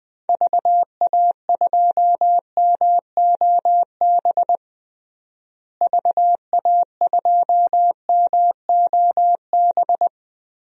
Morse code, heard as a keyed tone, about 20 wpm, 700 Hz. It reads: VA2MOB VA2MOB
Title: DXCC-100 Call Signs - 6 Characters - Repeated 3x - 20wpm